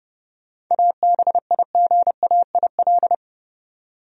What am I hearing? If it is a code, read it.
Morse code, heard as a keyed tone, about 32 wpm, 700 Hz. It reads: ABIGAIL
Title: USA Common Names - 30wpm